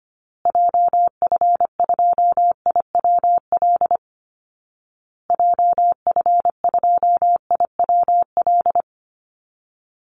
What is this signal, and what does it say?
Morse code, heard as a keyed tone, about 25 wpm, 700 Hz. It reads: JF2IWL JF2IWL
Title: Call Signs Repeated 2x - 25wpm